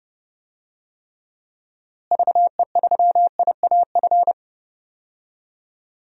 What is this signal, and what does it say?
Morse code, heard as a keyed tone, about 30 wpm, 700 Hz. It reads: VE3IAF
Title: DXCC-100 Call Signs - 6 Characters - Repeated 2x - 30wpm